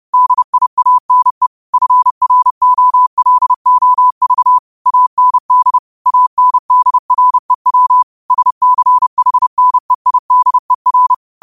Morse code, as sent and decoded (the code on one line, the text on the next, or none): -.. .. .- -. / ..-. .-. --- .-.. --- ...- / .- -. -.. / .- -. -.. .-. .-- / ... -.-. .... -. .. -.. .-.
DIANE FROLOV AND ANDREW SCHNEIDER